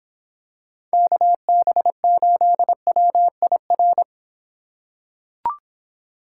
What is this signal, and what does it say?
Morse code, heard as a keyed tone, about 26 wpm, 700 Hz. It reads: KB8WIR